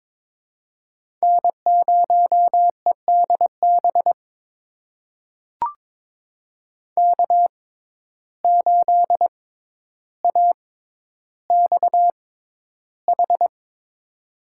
Morse code, heard as a keyed tone, about 21 wpm, spaced as for 11 wpm, 700 Hz. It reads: N0EDB K8AXH
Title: USA Call Signs - 5 Characters - 6x Character Spacing - 22wpm